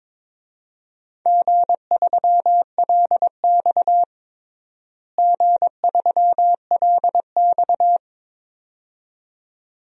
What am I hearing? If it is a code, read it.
Morse code, heard as a keyed tone, about 22 wpm, 700 Hz. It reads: G3LX G3LX
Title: DXCC-100 Call Signs - 4 Characters - Repeated 2x - 22wpm